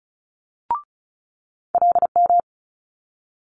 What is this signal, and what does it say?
Morse code, heard as a keyed tone, about 36 wpm, 700 Hz. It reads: LM
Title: Multi-letter Phonemes Quick Fire 35wpm